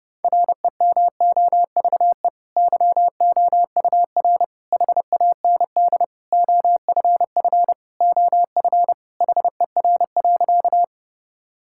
Morse code, heard as a keyed tone, about 30 wpm, 700 Hz. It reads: REMOVE YOUR HAND OFF OF HER.
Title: Sentences from Top 1000 Words - Review Part 5 - 30wpm